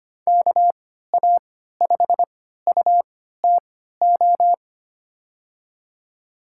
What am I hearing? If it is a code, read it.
Morse code, heard as a keyed tone, about 25 wpm, spaced as for 14 wpm, 700 Hz. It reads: KA5UTO
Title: USA Call Signs - 3x Character Spacing - 25wpm